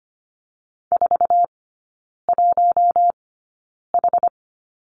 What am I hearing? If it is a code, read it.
Morse code, heard as a keyed tone, about 25 wpm, spaced as for 9 wpm, 690 Hz. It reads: 41H
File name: Lesson 14e - Number 2 - Three Characters 25wpm